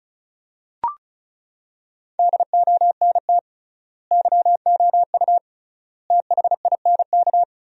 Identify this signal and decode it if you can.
Morse code, heard as a keyed tone, about 36 wpm, 700 Hz. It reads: DONT YOU THINK
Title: 3x Word-Spacing - Sentences from Top 300 Words - 35wpm